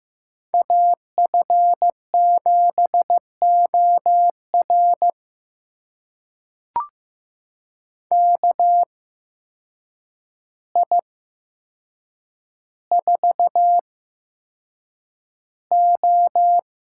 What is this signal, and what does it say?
Morse code, heard as a keyed tone, about 15 wpm, spaced as for 7 wpm, 700 Hz. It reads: AF7OR KI4O
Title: USA Call Signs - 8x Character Spacing - 15wpm